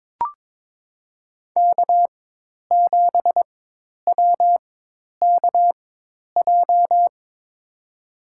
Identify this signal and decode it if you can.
Morse code, heard as a keyed tone, about 22 wpm, spaced as for 10 wpm, 700 Hz. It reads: K7WKJ
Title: USA Call Signs - 4x Character Spacing - 22wpm